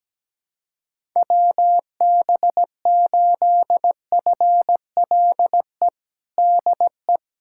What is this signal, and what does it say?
Morse code, heard as a keyed tone, about 17 wpm, 700 Hz. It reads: WB8FLE DE
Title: USA Callsign DE Callsign - 6 Characters - 17wpm